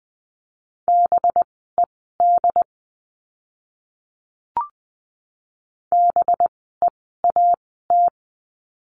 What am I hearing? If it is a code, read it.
Morse code, heard as a keyed tone, about 20 wpm, spaced as for 15 wpm, 700 Hz. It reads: BED BEAT
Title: 185 Lesson 30f - Letter b - Words 20wpm